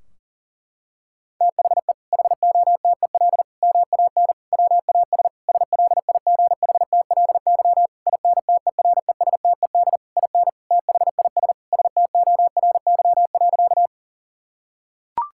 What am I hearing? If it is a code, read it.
Morse code, heard as a keyed tone, about 40 wpm, 700 Hz. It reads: THE HOTEL MAN WAS SLIGHTLY INTERESTED IN THIS STORY.
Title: Sentences from Top 1600 Words 40wpm